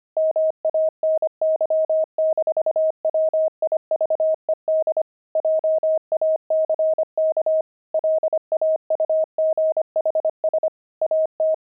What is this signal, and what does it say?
Morse code, heard as a keyed tone, about 25 wpm, 630 Hz. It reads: MANY-WIVED JACK LAUGHS AT
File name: pangram-fs-25-cs-25-wived